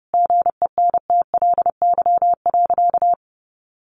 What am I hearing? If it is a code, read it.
Morse code, heard as a keyed tone, about 30 wpm, 700 Hz. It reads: GENTLY.
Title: Sentences from Top 1500 Words 30wpm